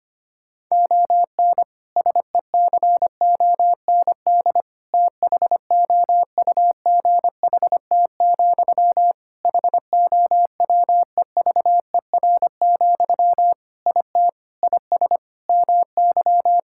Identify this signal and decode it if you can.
Morse code, heard as a keyed tone, about 25 wpm, 700 Hz. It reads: ON SECOND THOUGHT, HOWEVER, IT IS MY
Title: Sentences from Top 1600 Words 25wpm